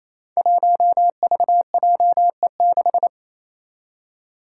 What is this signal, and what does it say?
Morse code, heard as a keyed tone, about 29 wpm, 700 Hz. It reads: 1VJE6
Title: Five Challenging Letter-Number Combinations 28wpm